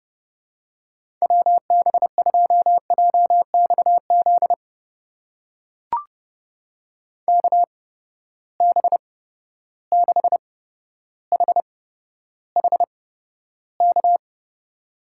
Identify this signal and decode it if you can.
Morse code, heard as a keyed tone, about 30 wpm, spaced as for 13 wpm, 700 Hz. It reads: WB2JXZ KB6HHK
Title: USA Call Signs - 6 Characters - 8x Character Spacing - 30wpm